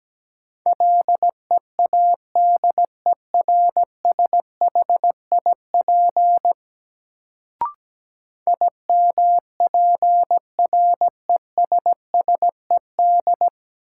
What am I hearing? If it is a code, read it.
Morse code, heard as a keyed tone, about 17 wpm, 700 Hz. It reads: LEADERSHIP IMPRESSED
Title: Sentences from Top 1600 Words - New Word Drill Alt - 17wpm